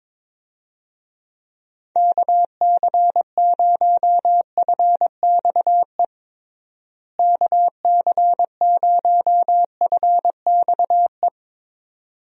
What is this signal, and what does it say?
Morse code, heard as a keyed tone, about 23 wpm, 700 Hz. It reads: KC0FXE KC0FXE
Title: USA Call Signs - 6 Characters - Repeated 2x - 22wpm